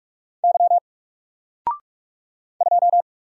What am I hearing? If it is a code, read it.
Morse code, heard as a keyed tone, about 44 wpm, 700 Hz. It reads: Y 2
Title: Single Character (Letter:Number:Punctuation:Prosign) 45wpm